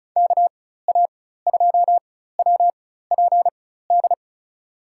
Morse code, heard as a keyed tone, about 34 wpm, spaced as for 16 wpm, 700 Hz. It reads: KA2WPD
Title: USA Call Signs - 6 Characters - 4x Character Spacing - 35wpm